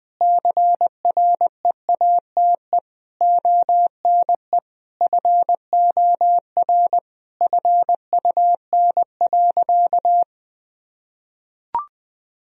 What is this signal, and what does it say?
Morse code, heard as a keyed tone, about 20 wpm, 700 Hz. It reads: CREATE ONE FOR FUN.
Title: Sentences from Top 1000 Words - Review Part 3 - 20wpm